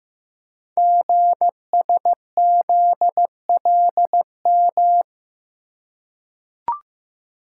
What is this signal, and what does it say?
Morse code, heard as a keyed tone, about 15 wpm, 690 Hz. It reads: GSZLM
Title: Code Groups - 5 Letters - 15wpm